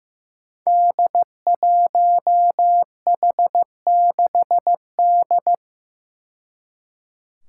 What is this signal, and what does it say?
Morse code, heard as a keyed tone, about 15 wpm, 700 Hz. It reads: D1H6D
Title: Five Challenging Letter-Number Combinations 15wpm